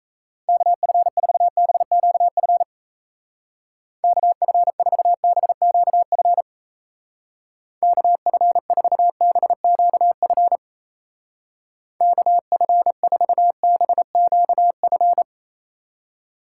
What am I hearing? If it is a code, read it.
Morse code, heard as a keyed tone, about 36 wpm, 700 Hz. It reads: KF4BQF KF4BQF KF4BQF KF4BQF